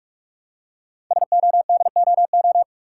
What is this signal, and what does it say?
Morse code, heard as a keyed tone, about 44 wpm, 700 Hz. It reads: SODOO
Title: Code Groups - 5 Letters - 45wpm